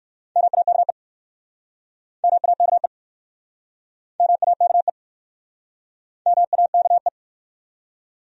Morse code, heard as a keyed tone, about 59 wpm, 700 Hz. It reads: MAKE MAKE MAKE MAKE